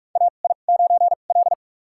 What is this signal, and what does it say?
Morse code, heard as a keyed tone, about 46 wpm, spaced as for 33 wpm, 700 Hz. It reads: AI9L
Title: USA Call Signs - 4 Characters - 2x Character Spacing - 45wpm